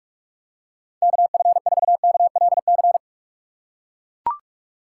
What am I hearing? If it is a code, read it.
Morse code, heard as a keyed tone, about 42 wpm, 700 Hz. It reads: KF4KLC
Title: USA Call Signs - 6 Characters - 6x Character Spacing - 45wpm